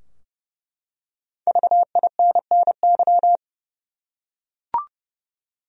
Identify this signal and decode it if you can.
Morse code, heard as a keyed tone, about 32 wpm, 700 Hz. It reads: VINNY
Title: USA Common Names - Max 5 Letters - 30wpm